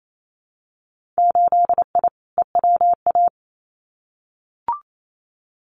Morse code, heard as a keyed tone, about 28 wpm, 700 Hz. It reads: 8I EWA
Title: Winter Field Day Exchanges Encore 28wpm